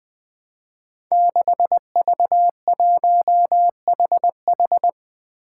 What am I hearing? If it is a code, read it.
Morse code, heard as a keyed tone, about 20 wpm, 700 Hz. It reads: 6V1HH
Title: Five Challenging Letter-Number Combinations 20wpm